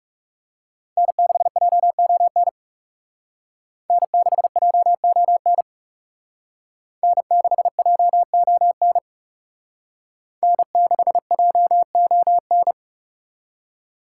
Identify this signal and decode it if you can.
Morse code, heard as a keyed tone, about 34 wpm, 700 Hz. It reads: N6JON N6JON N6JON N6JON
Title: Speed Racing - USA Call Signs 30wpm